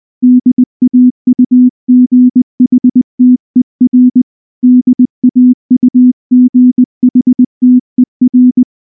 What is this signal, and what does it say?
Morse code, heard as a keyed tone, about 19 wpm, 250 Hz. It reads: DAUGHTER DAUGHTER
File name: morse / morse-training-20WPM-4